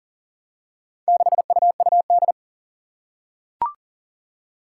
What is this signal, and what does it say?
Morse code, heard as a keyed tone, about 40 wpm, 700 Hz. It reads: 6UUD
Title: Four Challenging Letter-Number Combinations 40wpm